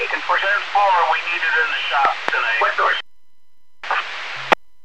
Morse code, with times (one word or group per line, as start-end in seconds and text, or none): none